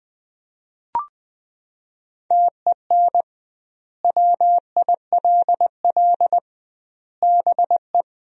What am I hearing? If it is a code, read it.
Morse code, heard as a keyed tone, about 20 wpm, 700 Hz. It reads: TEN WILL BE
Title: 2x Word-Spacing - Sentences from Top 400 Words - 20wpm